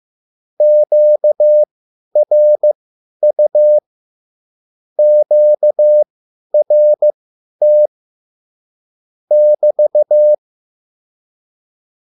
Morse code, heard as a keyed tone, about 15 wpm, spaced as for 10 wpm, 600 Hz. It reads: QRU QRT <BT>